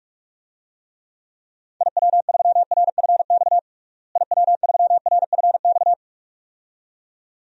Morse code, heard as a keyed tone, about 46 wpm, 700 Hz. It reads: IW3RFX IW3RFX
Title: DXCC-100 Call Signs - 6 Characters - Repeated 2x - 45wpm